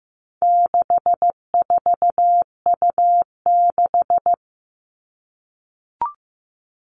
Morse code, heard as a keyed tone, about 15 wpm, 700 Hz. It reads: 64U6
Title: All Cut Numbers - 4-Digit 15wpm